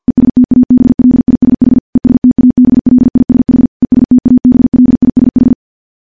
Morse code, heard as a keyed tone, about 50 wpm, 250 Hz. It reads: ESTABLISH ESTABLISH ESTABLISH
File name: morse-training-50WPM